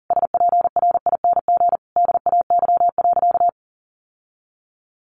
Morse code, heard as a keyed tone, about 40 wpm, 700 Hz. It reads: SPRING DAY.
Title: Sentences from Top 1200 Words 40wpm